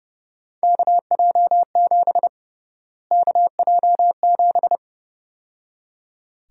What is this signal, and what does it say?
Morse code, heard as a keyed tone, about 30 wpm, 700 Hz. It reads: KJ7 KJ7
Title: USA Call Sign Prefixes - Repeated 2x - 30wpm